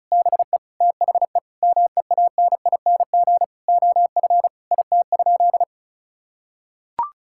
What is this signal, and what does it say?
Morse code, heard as a keyed tone, about 36 wpm, 700 Hz. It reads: BE THE MEANING OF IT?
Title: Sentences from Top 1000 Words - Review Part 6 - 35wpm